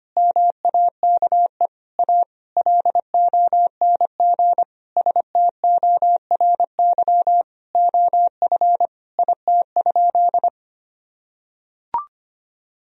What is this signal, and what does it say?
Morse code, heard as a keyed tone, about 25 wpm, 700 Hz. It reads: MAKE A LONG STORY OF IT?
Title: Sentences from Top 1000 Words - Review Part 1 - 25wpm